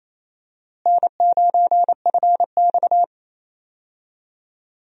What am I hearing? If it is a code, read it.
Morse code, heard as a keyed tone, about 27 wpm, 700 Hz. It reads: N9FX
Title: USA Call Signs - 4 Characters - Repeated 2x - 28wpm